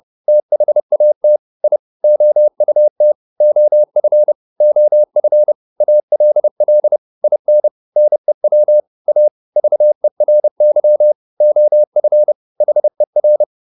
Morse code, heard as a keyed tone, about 30 wpm, 600 Hz. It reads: THAT I OUT OF OF ALL IN NEW A VERY OF HER